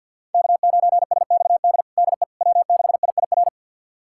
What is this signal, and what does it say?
Morse code, heard as a keyed tone, about 52 wpm, 700 Hz. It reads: K9SXD DE W6IIR
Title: USA Callsign DE Callsign - 5 Characters - 50wpm